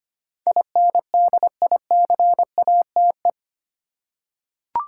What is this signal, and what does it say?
Morse code, heard as a keyed tone, about 25 wpm, 690 Hz. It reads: INDICATE
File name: 8 Letter Words - First 3 Letters Given - 25wpm